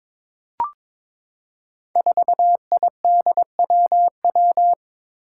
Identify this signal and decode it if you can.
Morse code, heard as a keyed tone, about 21 wpm, 700 Hz. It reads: 4IDWW
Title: Five Challenging Letter-Number Combinations 22wpm